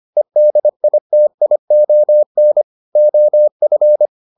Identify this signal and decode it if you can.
Morse code, heard as a keyed tone, about 25 wpm, 600 Hz. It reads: EDITION OF